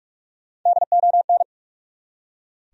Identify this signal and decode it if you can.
Morse code, heard as a keyed tone, about 45 wpm, 700 Hz. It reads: DON
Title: USA Common Names - Max 5 Letters - 45wpm